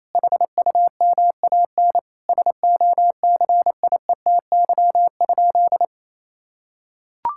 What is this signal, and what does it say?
Morse code, heard as a keyed tone, about 28 wpm, 700 Hz. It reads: HUMAN SOCIETY?
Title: Sentences from Top 1000 Words - Review Part 6 - 28wpm